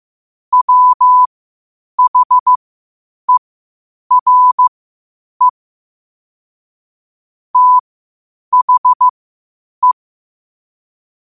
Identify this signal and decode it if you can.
Morse code, heard as a keyed tone, about 15 wpm, spaced as for 8 wpm, 1000 Hz. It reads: WHERE THE
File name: qotdfiles / 8wpm